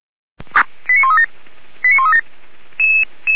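Five-tone selective call (ZVEI), heard as quick runs of five tones, tones of about 70 ms. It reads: 89137 89137
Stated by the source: ZVEI five-tone call received by SDR.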